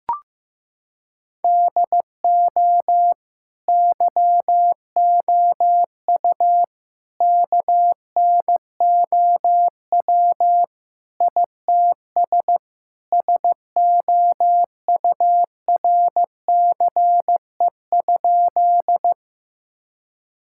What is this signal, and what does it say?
Morse code, heard as a keyed tone, about 16 wpm, 700 Hz. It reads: DO YOU KNOW ITS SOURCE?
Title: Sentences from Top 1000 Words - Review Part 5 - 15wpm